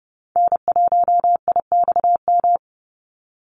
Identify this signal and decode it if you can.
Morse code, heard as a keyed tone, about 30 wpm, 700 Hz. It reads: N1IXM
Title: USA Call Signs 30wpm